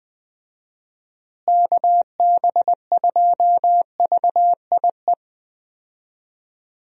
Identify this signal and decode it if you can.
Morse code, heard as a keyed tone, about 20 wpm, 700 Hz. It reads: KB2VIE
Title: USA Call Signs - 6 Characters - Repeated 2x - 20wpm